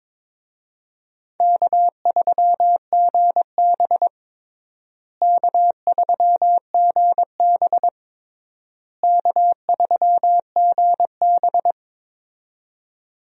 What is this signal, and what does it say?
Morse code, heard as a keyed tone, about 22 wpm, 700 Hz. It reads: K3GB K3GB K3GB
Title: USA Call Signs - 4 Characters - Repeated 3x - 22wpm